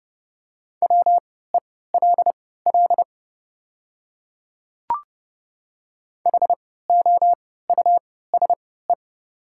Morse code, heard as a keyed tone, about 30 wpm, spaced as for 17 wpm, 700 Hz. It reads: WELL HOUSE
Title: 3x Character Spacing - Top 300 Words - 30wpm